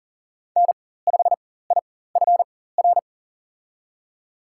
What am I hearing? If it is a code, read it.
Morse code, heard as a keyed tone, about 40 wpm, spaced as for 19 wpm, 700 Hz. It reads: N5IFR
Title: USA Call Signs - 5 Characters - 4x Character Spacing - 40wpm